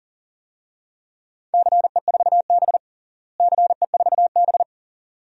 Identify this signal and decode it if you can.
Morse code, heard as a keyed tone, about 40 wpm, 700 Hz. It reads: CE4B CE4B
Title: DXCC-100 Call Signs - 4 Characters - Repeated 2x - 40wpm